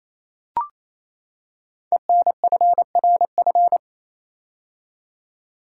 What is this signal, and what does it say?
Morse code, heard as a keyed tone, about 28 wpm, 700 Hz. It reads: ENFRF
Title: Code Groups - 5 Letters - 28wpm